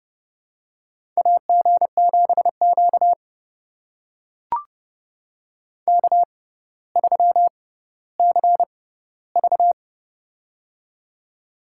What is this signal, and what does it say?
Morse code, heard as a keyed tone, about 30 wpm, spaced as for 15 wpm, 700 Hz. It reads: AG7Q K3CV
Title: USA Call Signs - 4 Characters - 6x Character Spacing - 30wpm